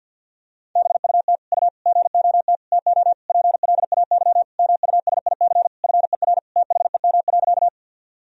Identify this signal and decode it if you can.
Morse code, heard as a keyed tone, about 47 wpm, 700 Hz. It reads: BUT U GOT TO PLAY MUSIC FER THEM.
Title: Sentences from Top 300 Words and Abbreviations 50wpm